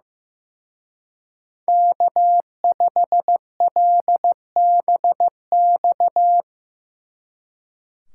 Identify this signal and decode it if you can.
Morse code, heard as a keyed tone, about 17 wpm, 700 Hz. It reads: K5LBX
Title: Speed Racing - USA 1x3 Call Signs 15wpm